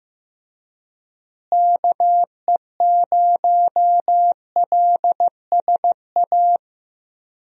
Morse code, heard as a keyed tone, about 15 wpm, 700 Hz. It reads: KE0LSA
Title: USA Call Signs - 6 Characters - Repeated 2x - 15wpm